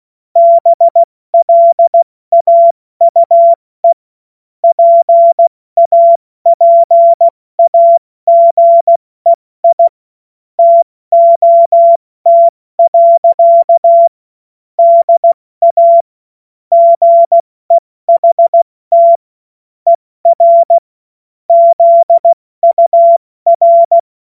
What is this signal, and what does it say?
Morse code, heard as a keyed tone, about 16 wpm, 680 Hz. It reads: BLAUE PAPAGEI TOT. DA GEHT ER ZUR
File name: Morsekurs / Witze / Witz 28. September 2021, 70 F80